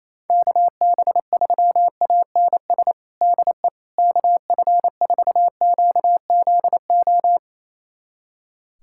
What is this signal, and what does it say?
Morse code, heard as a keyed tone, about 28 wpm, 700 Hz. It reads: KB3ANS DE KF4QZO
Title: USA Callsign DE Callsign - 6 Characters - 28wpm